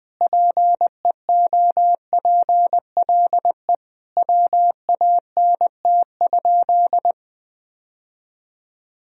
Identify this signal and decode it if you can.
Morse code, heard as a keyed tone, about 21 wpm, 700 Hz. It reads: PEOPLE WANT?
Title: Sentences from Top 1000 Words - Review Part 1 - 20wpm